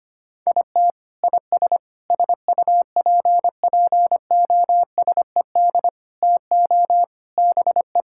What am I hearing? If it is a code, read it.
Morse code, heard as a keyed tone, about 25 wpm, 700 Hz. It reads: IT IS SUPPOSED TO BE
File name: practice / 25wpm / Sentences with 3-Word Phrases - 25wpm